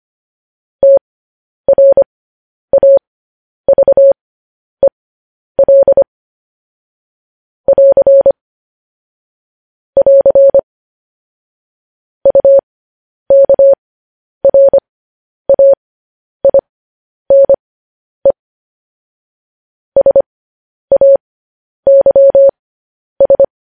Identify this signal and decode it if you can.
Morse code, heard as a keyed tone, about 25 wpm, spaced as for 10 wpm, 550 Hz. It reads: TRAVEL <AR> <AR> UKRAINE SAYS
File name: podcast / News headlines-1x-25x10